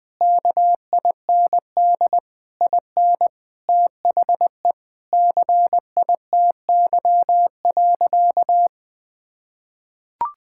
Morse code, heard as a keyed tone, about 21 wpm, 700 Hz. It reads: KIND IN THE CITY.
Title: Sentences with 6-Word Phrases - 20wpm